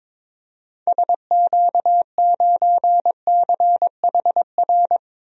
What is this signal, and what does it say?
Morse code, heard as a keyed tone, about 21 wpm, 700 Hz. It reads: SQ9CHR
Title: DXCC-100 Call Signs - 6 Characters - 22wpm